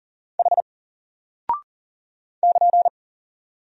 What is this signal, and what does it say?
Morse code, heard as a keyed tone, about 39 wpm, 700 Hz. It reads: H <KN>
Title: Single Character (Letter:Number:Punctuation:Prosign) Alt 40wpm